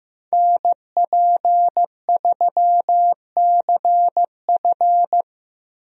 Morse code, heard as a keyed tone, about 16 wpm, 700 Hz. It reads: NP3CF
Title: USA Call Signs - 5 Characters - Repeated 2x - 15wpm